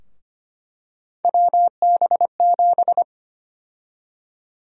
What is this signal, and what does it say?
Morse code, heard as a keyed tone, about 25 wpm, 700 Hz. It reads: WB7
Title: USA Call Sign Prefixes - Repeated 2x - 25wpm